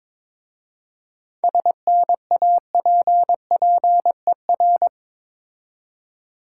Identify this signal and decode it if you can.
Morse code, heard as a keyed tone, about 22 wpm, 700 Hz. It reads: SNAPPER
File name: Difficult Words with Hints - 22wpm